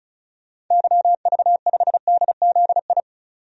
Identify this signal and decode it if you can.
Morse code, heard as a keyed tone, about 36 wpm, 700 Hz. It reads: YV5DZI
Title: DXCC-100 Call Signs - 6 Characters - 35wpm